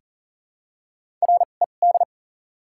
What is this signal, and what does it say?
Morse code, heard as a keyed tone, about 40 wpm, spaced as for 29 wpm, 700 Hz. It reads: RED
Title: 2x Character Spacing - Top 500 Words - 40wpm